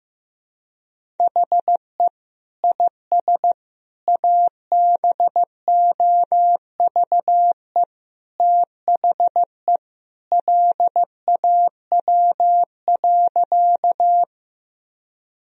Morse code, heard as a keyed tone, about 15 wpm, 700 Hz. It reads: HE IS ABOVE THE LAW.
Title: Sentences from Top 1000 Words - Review Part 3 - 15wpm